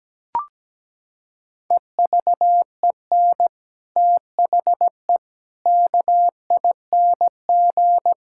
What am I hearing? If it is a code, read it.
Morse code, heard as a keyed tone, about 17 wpm, 700 Hz. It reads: EVEN THE KING
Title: Sentences from Top 1000 Words - Review Part 6 - 17wpm